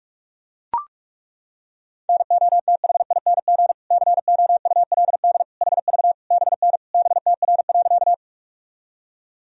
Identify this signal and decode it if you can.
Morse code, heard as a keyed tone, about 41 wpm, 690 Hz. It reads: NOTHING COULD HV BN BTR.
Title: Sentences from Top 500 Words and Abbreviations 45wpm